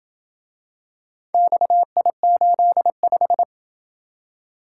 Morse code, heard as a keyed tone, about 27 wpm, 700 Hz. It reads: XI85